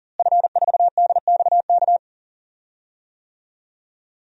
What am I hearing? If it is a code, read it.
Morse code, heard as a keyed tone, about 37 wpm, 700 Hz. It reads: F4DXK
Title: DXCC-100 Call Signs - 5 Characters - Repeated 2x - 40wpm